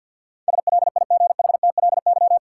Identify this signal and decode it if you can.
Morse code, heard as a keyed tone, about 47 wpm, 700 Hz. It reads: SLIGHTLY
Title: Sentences from Top 1300 Words - New Word Drill Alt - 50wpm